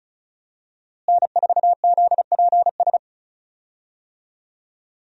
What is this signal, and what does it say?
Morse code, heard as a keyed tone, about 34 wpm, 700 Hz. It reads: N4ZPS
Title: USA Call Signs - 5 Characters - Repeated 3x - 35wpm